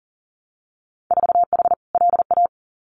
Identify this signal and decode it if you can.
Morse code, heard as a keyed tone, about 39 wpm, 700 Hz. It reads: 4H LA
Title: Winter Field Day Exchanges Encore 40wpm